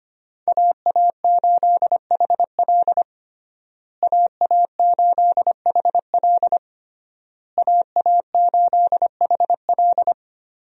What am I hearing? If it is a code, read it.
Morse code, heard as a keyed tone, about 25 wpm, 700 Hz. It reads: AA8HL AA8HL AA8HL
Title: USA Call Signs - 5 Characters - Repeated 3x - 25wpm